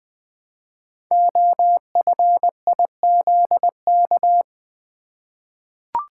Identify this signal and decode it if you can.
Morse code, heard as a keyed tone, about 20 wpm, 700 Hz. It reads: OFIZK
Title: Code Groups - 5 Letters - 20wpm